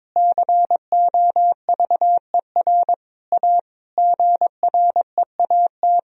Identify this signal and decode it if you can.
Morse code, heard as a keyed tone, about 22 wpm, 700 Hz. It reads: COVER A GREAT
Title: Sentences from Top 1000 Words - Review Part 2 - 22wpm